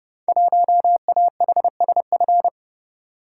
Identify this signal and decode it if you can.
Morse code, heard as a keyed tone, about 30 wpm, 700 Hz. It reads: 1AHSF
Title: Code Groups - 5 Letter:Numbers - 30wpm